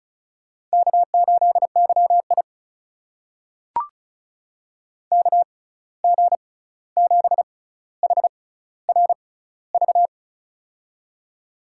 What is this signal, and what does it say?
Morse code, heard as a keyed tone, about 36 wpm, spaced as for 16 wpm, 700 Hz. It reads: K8YI KG7HRV
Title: USA Call Signs - 6x Character Spacing - 35wpm